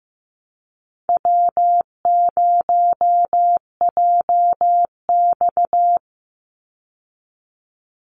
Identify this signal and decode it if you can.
Morse code, heard as a keyed tone, about 15 wpm, 700 Hz. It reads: W0JX
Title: Call Signs Repeated 2x - 15wpm